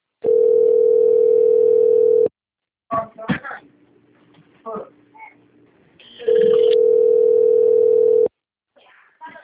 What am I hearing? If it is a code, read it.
Ringback tone (North America), 440 + 480 Hz, with speech.
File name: rg-1600-unknown- - - .57735